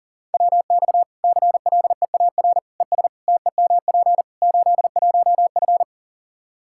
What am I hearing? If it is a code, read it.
Morse code, heard as a keyed tone, about 40 wpm, 700 Hz. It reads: WX CLEAR ES TEMP 81F
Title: Practice QSO - Exchange 2 - Info - 40wpm